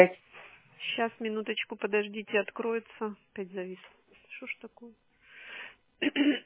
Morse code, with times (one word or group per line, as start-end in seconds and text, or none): none